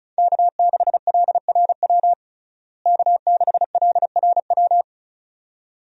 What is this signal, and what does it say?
Morse code, heard as a keyed tone, about 36 wpm, 700 Hz. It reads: K6LRW K6LRW
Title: USA Call Signs - 5 Characters - Repeated 2x - 35wpm